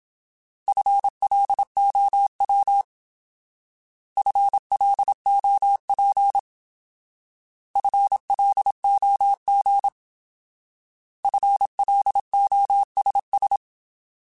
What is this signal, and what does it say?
Morse code, heard as a keyed tone, about 27 wpm, 800 Hz. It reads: FLOW FLOP FLOG FLOSS
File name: Alphabetical / Alphabetical mp3s / Fs @25